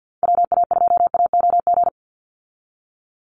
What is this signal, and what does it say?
Morse code, heard as a keyed tone, about 50 wpm, 700 Hz. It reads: WA2AOG
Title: Call Signs Repeated 2x - 50wpm